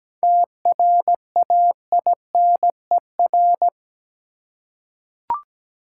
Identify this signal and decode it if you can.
Morse code, heard as a keyed tone, about 17 wpm, 700 Hz. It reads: TRAINER
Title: Seven Letter Words - 17wpm